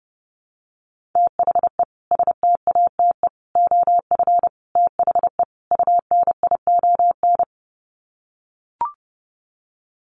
Morse code, heard as a keyed tone, about 30 wpm, 700 Hz. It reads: THE STATE OF THE UNION
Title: Sets of 5 Words Encore 30wpm